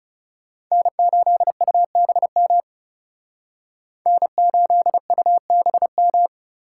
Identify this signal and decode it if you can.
Morse code, heard as a keyed tone, about 38 wpm, 700 Hz. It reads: N8UBM N8UBM